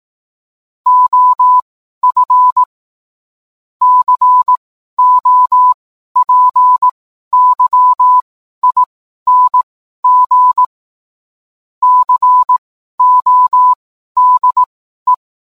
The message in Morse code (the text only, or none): OF COPYING CODE